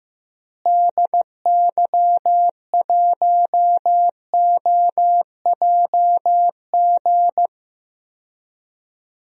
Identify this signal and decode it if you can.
Morse code, heard as a keyed tone, about 15 wpm, 700 Hz. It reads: DY1OJG
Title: DXCC-100 Call Signs - 6 Characters - Repeated 2x - 15wpm